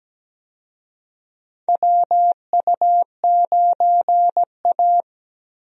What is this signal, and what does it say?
Morse code, heard as a keyed tone, about 17 wpm, 700 Hz. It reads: WU9A
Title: USA Call Signs - 4 Characters - Repeated 2x - 17wpm